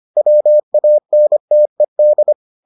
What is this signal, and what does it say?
Morse code, heard as a keyed tone, about 25 wpm, 600 Hz. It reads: WANTED